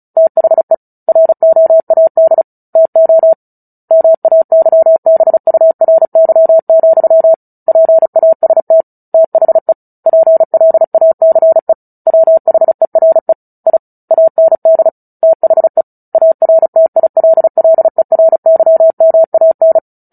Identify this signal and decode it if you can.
Morse code, heard as a keyed tone, about 34 wpm, 650 Hz. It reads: THE ROAD TO MAYBURY, PAST THE PLACE WHERE I AND THE ARTILLERYMAN
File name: Book Two / 09-Chapter 9 - 35 WPM